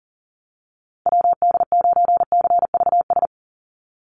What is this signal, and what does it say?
Morse code, heard as a keyed tone, about 40 wpm, 700 Hz. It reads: WD9CVS
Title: USA Call Signs 40wpm